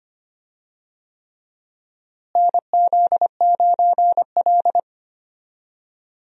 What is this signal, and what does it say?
Morse code, heard as a keyed tone, about 25 wpm, 700 Hz. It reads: NZ9L
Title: USA Call Signs - 4 Characters - Repeated 3x - 25wpm